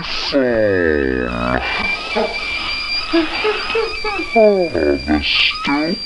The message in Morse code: none